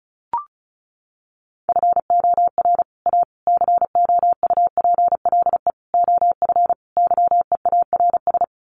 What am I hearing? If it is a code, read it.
Morse code, heard as a keyed tone, about 34 wpm, 700 Hz. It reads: FOR A COUPLE OF YEARS
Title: Sets of 5 Words Encore 35wpm